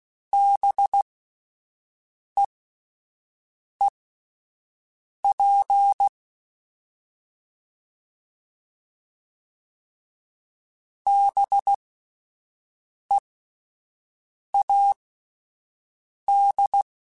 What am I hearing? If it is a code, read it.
Morse code, heard as a keyed tone, about 16 wpm, spaced as for 5 wpm, 800 Hz. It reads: BEEP BEAD
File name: Spaced Alphas / Spaced Alphas mp3s / Bs sp @10